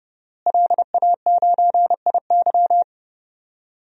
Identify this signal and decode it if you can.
Morse code, heard as a keyed tone, about 30 wpm, 700 Hz. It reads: LA9IY
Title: DXCC-100 Call Signs - 5 Characters - Repeated 2x - 30wpm